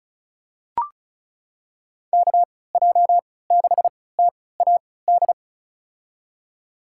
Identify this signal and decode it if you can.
Morse code, heard as a keyed tone, about 36 wpm, spaced as for 20 wpm, 700 Hz. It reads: KJ6TAD
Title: USA Call Signs - 6 Characters - 3x Character Spacing - 35wpm